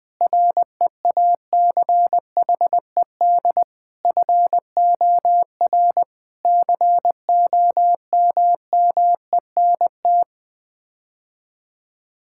Morse code, heard as a keyed tone, about 20 wpm, 700 Hz. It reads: REACHED FOR COMMENT
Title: Sets of 6 Words - 20wpm